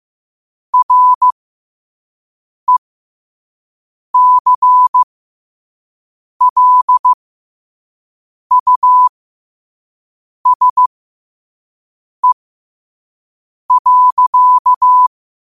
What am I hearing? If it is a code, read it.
Morse code, heard as a keyed tone, about 15 wpm, spaced as for 5 wpm, 1000 Hz. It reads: RECLUSE.
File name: qotdfiles / 5wpm